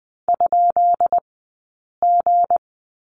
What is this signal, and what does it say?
Morse code, heard as a keyed tone, about 20 wpm, spaced as for 8 wpm, 700 Hz. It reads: ?G
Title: 211 Lesson 34d - Number 8 - Two Characters 20wpm